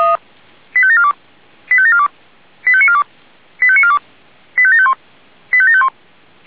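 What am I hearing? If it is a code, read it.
Five-tone selective call (ZVEI), heard as quick runs of five tones, tones of about 70 ms. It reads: 85742 85742 85942 85942 85751 85751
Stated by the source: ZVEI five-tone selective call received on SDR.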